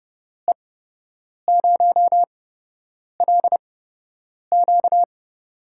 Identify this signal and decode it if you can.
Morse code, heard as a keyed tone, about 30 wpm, spaced as for 8 wpm, 700 Hz. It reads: E0LQ
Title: USA Call Signs - 5 Characters - 8x Character Spacing - 30wpm